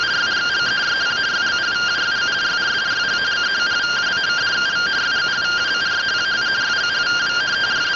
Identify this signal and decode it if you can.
Two-tone teleprinter signal, about 170 Hz shift.